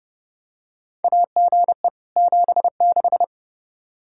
Morse code, heard as a keyed tone, about 30 wpm, 700 Hz. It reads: AGE 76
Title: Practice QSO - Exchange 3 - More Info - 30wpm